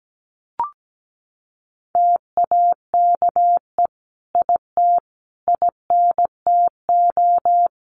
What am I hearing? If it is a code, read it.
Morse code, heard as a keyed tone, about 17 wpm, 700 Hz. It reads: TAKE IT INTO
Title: Sentences from Top 1500 Words 17wpm